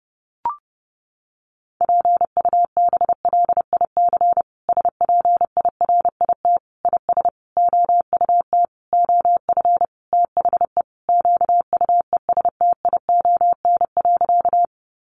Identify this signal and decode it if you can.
Morse code, heard as a keyed tone, about 30 wpm, 700 Hz. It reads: PUBLIC SPIRIT IS OUT OF THE QUESTION.
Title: Sentences from Top 1400 Words 30wpm